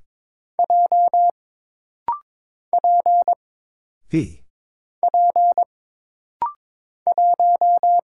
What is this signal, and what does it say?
Morse code, heard as a keyed tone, about 22 wpm, 700 Hz. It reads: J P P 1
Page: This rapid-fire timing is geared to push your ICR (Instant Character Recognition) skills.